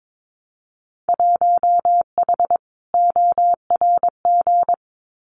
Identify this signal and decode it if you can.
Morse code, heard as a keyed tone, about 22 wpm, 700 Hz. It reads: 1H ORG
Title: Winter Field Day Exchanges 22wpm